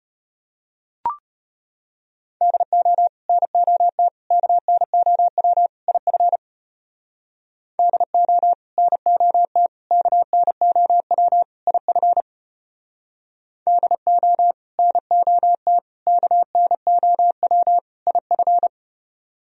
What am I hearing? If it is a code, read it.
Morse code, heard as a keyed tone, about 33 wpm, 700 Hz. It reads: DO NOT KNOW IF DO NOT KNOW IF DO NOT KNOW IF